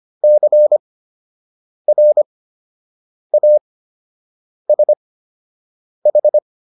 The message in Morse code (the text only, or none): CRASH